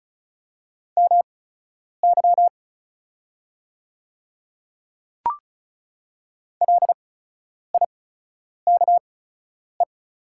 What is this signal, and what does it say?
Morse code, heard as a keyed tone, about 34 wpm, spaced as for 10 wpm, 700 Hz. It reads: MY LIKE
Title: 8x Character Spacing - Top 100 Words - 35wpm